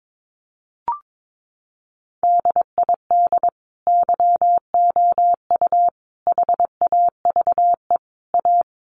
Morse code, heard as a keyed tone, about 22 wpm, 700 Hz. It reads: DID YOU HAVE A
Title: Sentences from Top 1100 Words 22wpm